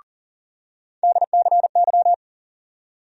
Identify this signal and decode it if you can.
Morse code, heard as a keyed tone, about 40 wpm, 700 Hz. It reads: DCY
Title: USA Call Sign Postfixes - 3 Letters 40wpm